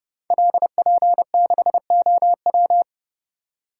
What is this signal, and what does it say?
Morse code, heard as a keyed tone, about 30 wpm, 700 Hz. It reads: LP6OW
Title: Code Groups - 5 Letter:Numbers - 30wpm